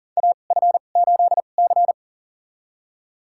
Morse code, heard as a keyed tone, about 39 wpm, spaced as for 29 wpm, 700 Hz. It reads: AF8C
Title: USA Call Signs - 4 Characters - 2x Character Spacing - 40wpm